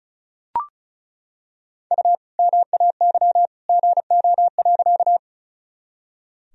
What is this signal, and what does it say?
Morse code, heard as a keyed tone, about 34 wpm, 700 Hz. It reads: U MAY GO.
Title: Sentences from Top 100 Words and Abbreviations 35wpm